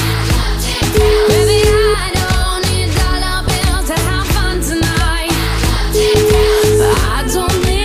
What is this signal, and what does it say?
Ringback tone (Europe, 425 Hz), with noise.